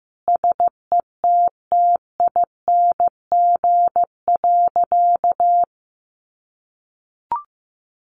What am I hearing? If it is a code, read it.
Morse code, heard as a keyed tone, about 15 wpm, 700 Hz. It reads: SETTING.
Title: Sentences from Top 1300 Words 15wpm